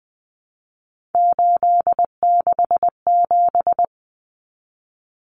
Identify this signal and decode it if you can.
Morse code, heard as a keyed tone, about 20 wpm, 700 Hz. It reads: 867
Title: Common Cut Numbers - 3-Digit 20wpm